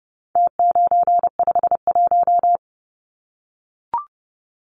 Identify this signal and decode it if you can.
Morse code, heard as a keyed tone, about 30 wpm, 700 Hz. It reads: T951
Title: Numbers - Contest Style - 4-Digit 30wpm